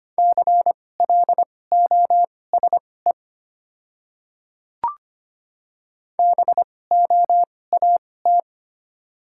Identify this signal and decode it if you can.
Morse code, heard as a keyed tone, about 25 wpm, spaced as for 18 wpm, 700 Hz. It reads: CLOSE BOAT
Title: 2x Character Spacing - Top 500 Words - 25wpm